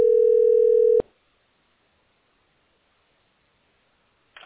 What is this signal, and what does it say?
Ringback tone (North America), 440 + 480 Hz.